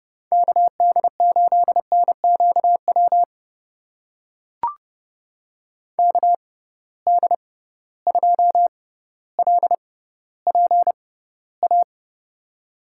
Morse code, heard as a keyed tone, about 30 wpm, spaced as for 15 wpm, 700 Hz. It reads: KD8NQW KD2LPA
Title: USA Call Signs - 6 Characters - 6x Character Spacing - 30wpm